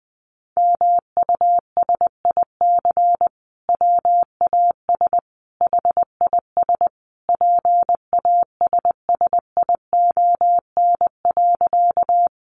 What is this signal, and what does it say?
Morse code, heard as a keyed tone, about 20 wpm, 700 Hz. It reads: MUSIC WAS HIS PASSION.
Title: Sentences from Top 1300 Words 20wpm